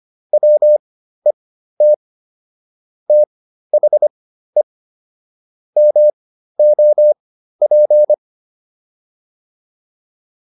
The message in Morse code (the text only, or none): WET THE MOP